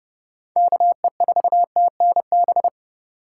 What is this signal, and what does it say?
Morse code, heard as a keyed tone, about 31 wpm, 700 Hz. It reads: KE4TNB
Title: USA Call Signs - 3x Character Spacing - 30wpm